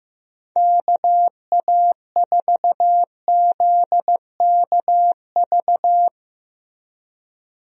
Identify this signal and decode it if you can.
Morse code, heard as a keyed tone, about 16 wpm, 700 Hz. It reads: KA4ZKV
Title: USA Call Signs - 6 Characters - Repeated 3x - 15wpm